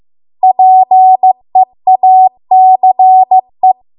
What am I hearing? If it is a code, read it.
Morse code, heard as a keyed tone, about 15 wpm, 750 Hz. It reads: PEACE